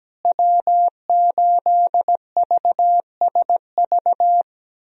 Morse code, heard as a keyed tone, about 17 wpm, 700 Hz. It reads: W8VSV
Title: USA Call Signs - 5 Characters - Repeated 2x - 17wpm